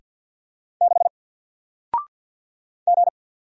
Morse code, heard as a keyed tone, about 48 wpm, 700 Hz. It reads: 6 G
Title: Single Letter-Number Alt 50wpm